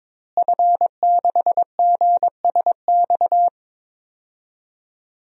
Morse code, heard as a keyed tone, about 22 wpm, 700 Hz. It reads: F6GSX
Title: DXCC-100 Call Signs - 5 Characters - Repeated 2x - 22wpm